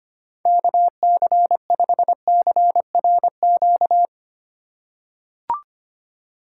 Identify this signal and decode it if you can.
Morse code, heard as a keyed tone, about 25 wpm, 700 Hz. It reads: KC5CRQ
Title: USA Call Signs - 6 Characters - 3x Character Spacing - 25wpm